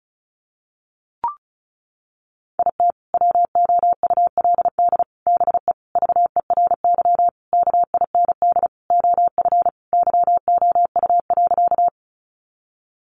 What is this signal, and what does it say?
Morse code, heard as a keyed tone, about 34 wpm, 700 Hz. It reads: IT WOULD BE VERY KIND OF YOU.
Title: Sentences from Top 200 Words 35wpm